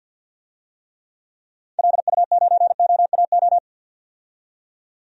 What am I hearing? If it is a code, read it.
Morse code, heard as a keyed tone, about 50 wpm, 700 Hz. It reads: LU9OAO
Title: DXCC-100 Call Signs - 6 Characters - 50wpm